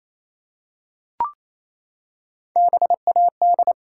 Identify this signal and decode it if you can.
Morse code, heard as a keyed tone, about 29 wpm, 700 Hz. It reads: BAD
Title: Three Letter Words - 28wpm